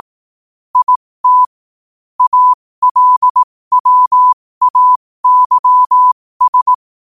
Morse code, heard as a keyed tone, about 18 wpm, spaced as for 15 wpm, 1000 Hz. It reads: IT ALWAYS
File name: qotdfiles / 15wpm